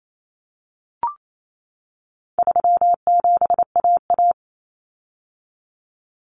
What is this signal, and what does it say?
Morse code, heard as a keyed tone, about 28 wpm, 700 Hz. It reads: 37AA
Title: Common Cut Numbers - 4-Digit 28wpm